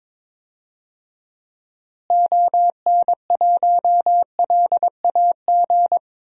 Morse code, heard as a keyed tone, about 22 wpm, 700 Hz. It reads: ON1LAG
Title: DXCC-100 Call Signs - 6 Characters - Repeated 3x - 22wpm